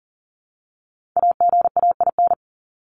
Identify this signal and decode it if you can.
Morse code, heard as a keyed tone, about 40 wpm, 700 Hz. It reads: AGAIN
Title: Top 300 Words 40wpm